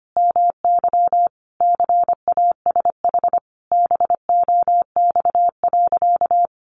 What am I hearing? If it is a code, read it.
Morse code, heard as a keyed tone, about 25 wpm, 700 Hz. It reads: MY CASH BOX.
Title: Sentences from Top 700 Words 25wpm